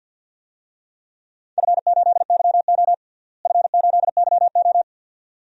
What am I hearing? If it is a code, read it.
Morse code, heard as a keyed tone, about 48 wpm, 700 Hz. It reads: F8YO F8YO
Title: DXCC-100 Call Signs - 4 Characters - Repeated 2x - 50wpm